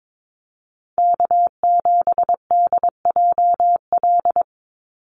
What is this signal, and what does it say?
Morse code, heard as a keyed tone, about 22 wpm, 700 Hz. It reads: K7DJL
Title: NA SOTA Chaser Callsigns 22wpm